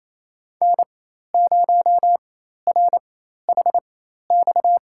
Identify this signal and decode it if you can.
Morse code, heard as a keyed tone, about 28 wpm, spaced as for 13 wpm, 700 Hz. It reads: N0RHX
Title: USA Call Signs - 5 Characters - 4x Character Spacing - 28wpm